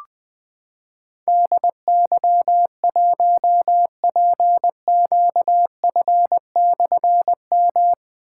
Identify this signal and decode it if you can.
Morse code, heard as a keyed tone, about 20 wpm, 700 Hz. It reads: DY1PQF/M
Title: Difficult Call Signs 20wpm